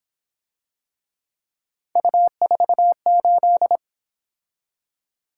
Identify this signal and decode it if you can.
Morse code, heard as a keyed tone, about 26 wpm, 700 Hz. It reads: U48
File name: Speed-Racing - Three Letter-Number Combinations - 22wpm